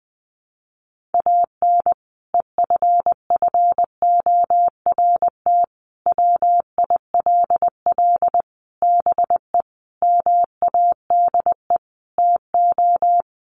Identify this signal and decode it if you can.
Morse code, heard as a keyed tone, about 20 wpm, 700 Hz. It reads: AN EFFORT WILL BE MADE TO
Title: Sentences from Top 1500 Words 20wpm